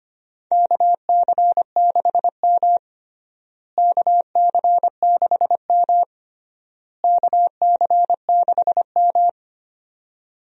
Morse code, heard as a keyed tone, about 25 wpm, 700 Hz. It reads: KC6M KC6M KC6M
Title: USA Call Signs - 4 Characters - Repeated 3x - 25wpm